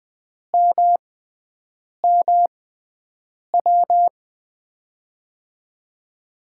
Morse code, heard as a keyed tone, about 20 wpm, spaced as for 7 wpm, 700 Hz. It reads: MMW
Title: USA Call Signs - 6 Characters - 6x Character Spacing - 20wpm